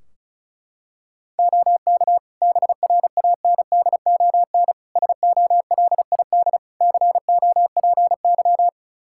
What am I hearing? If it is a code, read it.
Morse code, heard as a keyed tone, about 34 wpm, 700 Hz. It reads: OK BRANDON SOLID COPY
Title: K1USN SST QSO Practice 35wpm